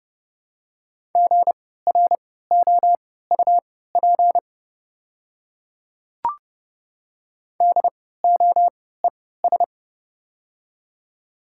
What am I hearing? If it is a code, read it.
Morse code, heard as a keyed tone, about 32 wpm, spaced as for 17 wpm, 700 Hz. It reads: GROUP DOES
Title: 3x Character Spacing - Top 300 Words - 30wpm